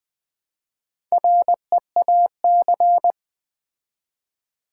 Morse code, heard as a keyed tone, about 20 wpm, 700 Hz. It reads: REAC
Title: Build A Word - 4 Letters At A Time - 8-Letter Words - 20wpm